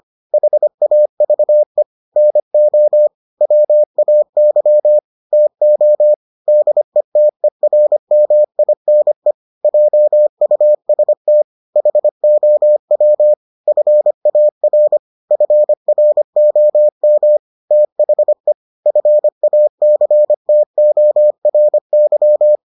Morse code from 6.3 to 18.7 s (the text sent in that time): DETERMINE JUST HOW FAR FROM THE